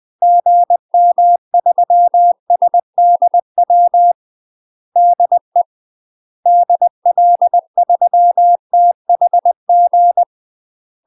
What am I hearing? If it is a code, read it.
Morse code, heard as a keyed tone, about 20 wpm, 700 Hz. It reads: GM3SDW DE DL3THG
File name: UK MASTER QSOs TRAINING / 20WPM / UK MASTER QSOs 0026